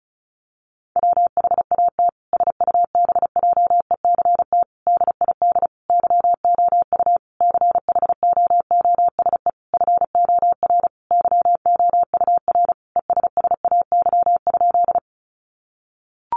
Morse code, heard as a keyed tone, about 34 wpm, 700 Hz. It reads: WHAT SUBJECT DID YOU CHOOSE FOR YOUR ESSAY?
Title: Sentences from Top 1500 Words 35wpm